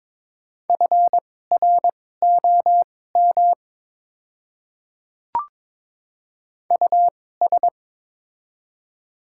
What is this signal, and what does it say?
Morse code, heard as a keyed tone, about 22 wpm, spaced as for 16 wpm, 700 Hz. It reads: FROM US
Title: 2x Character Spacing - Top 300 Words - 22wpm